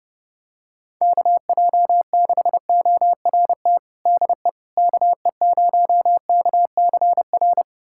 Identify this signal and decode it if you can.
Morse code, heard as a keyed tone, about 32 wpm, 700 Hz. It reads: KJ6ORT DE KE0KCR
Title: USA Callsign DE Callsign - 6 Characters - 30wpm